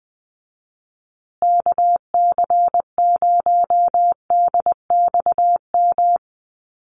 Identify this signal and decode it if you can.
Morse code, heard as a keyed tone, about 20 wpm, 700 Hz. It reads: KC0DXM
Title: USA Call Signs Repeated 2x 20wpm